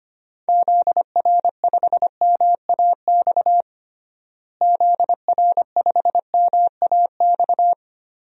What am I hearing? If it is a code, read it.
Morse code, heard as a keyed tone, about 25 wpm, 700 Hz. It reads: ZR5MAX ZR5MAX
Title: DXCC-100 Call Signs - 6 Characters - Repeated 3x - 25wpm